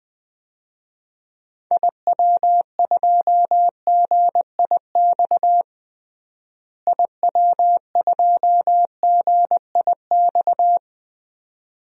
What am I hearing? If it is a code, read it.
Morse code, heard as a keyed tone, about 20 wpm, 700 Hz. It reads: IW2GIX IW2GIX
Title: DXCC-100 Call Signs - 6 Characters - Repeated 2x - 20wpm